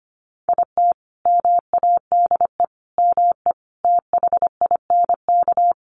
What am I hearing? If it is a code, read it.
Morse code, heard as a keyed tone, about 25 wpm, 690 Hz. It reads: IT MADE ME THINK
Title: Sentences from Top 1100 Words 25wpm